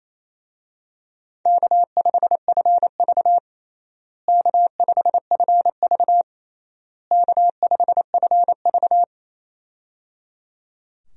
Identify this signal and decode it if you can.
Morse code, heard as a keyed tone, about 28 wpm, 700 Hz. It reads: K5FV K5FV K5FV
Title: USA Call Signs - 4 Characters - Repeated 3x - 28wpm